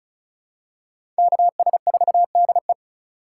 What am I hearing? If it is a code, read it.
Morse code, heard as a keyed tone, about 34 wpm, 690 Hz. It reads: KS4DE
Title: USA Call Signs - 4x Character Spacing - 35wpm